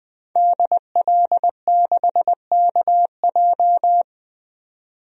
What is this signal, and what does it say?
Morse code, heard as a keyed tone, about 20 wpm, 700 Hz. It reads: DL6KJ
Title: DXCC-100 Call Signs - 5 Characters - 20wpm